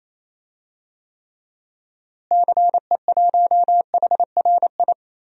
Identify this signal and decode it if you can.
Morse code, heard as a keyed tone, about 28 wpm, 700 Hz. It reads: CE1HRI
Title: DXCC-100 Call Signs - 6 Characters - Repeated 2x - 28wpm